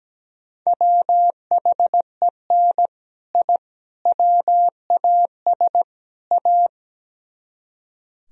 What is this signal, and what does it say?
Morse code, heard as a keyed tone, about 18 wpm, 700 Hz. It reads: WHEN I WAS A
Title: Speed-Racing - Sets of 4 Words 17wpm